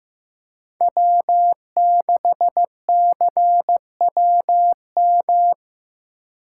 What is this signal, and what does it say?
Morse code, heard as a keyed tone, about 16 wpm, 700 Hz. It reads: W6CWM
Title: K1USN SST QSO Practice - 15wpm